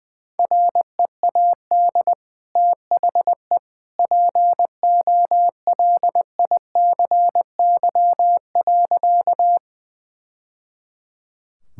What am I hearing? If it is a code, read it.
Morse code, heard as a keyed tone, about 21 wpm, 700 Hz. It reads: READ THE POLICY.
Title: Sentences from Top 1000 Words - Review Part 5 - 20wpm